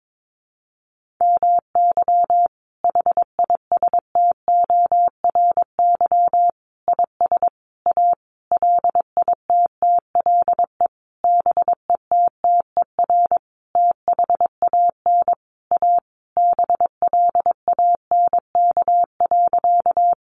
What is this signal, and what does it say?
Morse code, heard as a keyed tone, about 22 wpm, 700 Hz. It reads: MY HISTORY IS A LITTLE BETTER THAN A BLANK.
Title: Sentences from Top 1500 Words 22wpm